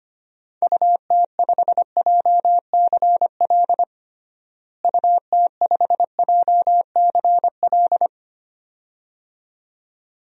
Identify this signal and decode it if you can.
Morse code, heard as a keyed tone, about 25 wpm, 700 Hz. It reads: UT5JCL UT5JCL
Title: DXCC-100 Call Signs - 6 Characters - Repeated 2x - 25wpm